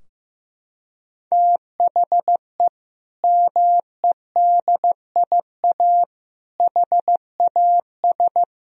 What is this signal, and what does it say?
Morse code, heard as a keyed tone, about 15 wpm, 700 Hz. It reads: THE MEDIA HAS
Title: Sentences from Top 1000 Words - Review Part 4 - 15wpm